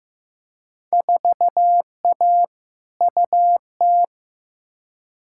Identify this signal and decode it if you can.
Morse code, heard as a keyed tone, about 15 wpm, 700 Hz. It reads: 4A UT
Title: Field Day Exchanges - 15wpm